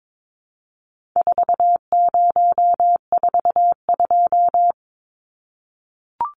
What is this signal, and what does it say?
Morse code, heard as a keyed tone, about 22 wpm, 700 Hz. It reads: 4042
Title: Numbers - 4-Digit 22wpm